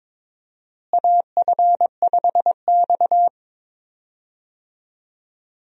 Morse code, heard as a keyed tone, about 22 wpm, 700 Hz. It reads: AF5X
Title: USA Call Signs - 4 Characters - Repeated 3x - 22wpm